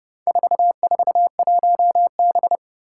Morse code, heard as a keyed tone, about 31 wpm, 700 Hz. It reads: 441B
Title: Four Challenging Letter-Number Combinations 30wpm